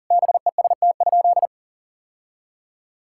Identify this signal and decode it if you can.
Morse code, heard as a keyed tone, about 40 wpm, 700 Hz. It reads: BEST?
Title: Sentences from Top 1000 Words - Review Part 5 - 40wpm